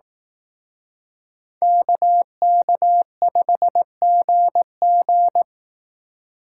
Morse code, heard as a keyed tone, about 18 wpm, 700 Hz. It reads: KK5GG
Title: Speed Racing - USA 2x2 Call Signs 15wpm